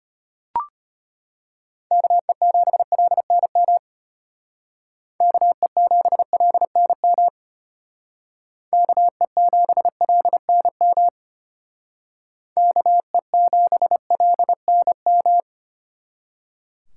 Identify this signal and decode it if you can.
Morse code, heard as a keyed tone, about 32 wpm, 700 Hz. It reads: KE7LNM KE7LNM KE7LNM KE7LNM